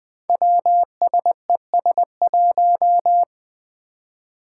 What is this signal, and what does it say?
Morse code, heard as a keyed tone, about 20 wpm, 700 Hz. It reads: WSES1
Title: Five Challenging Letter-Number Combinations 20wpm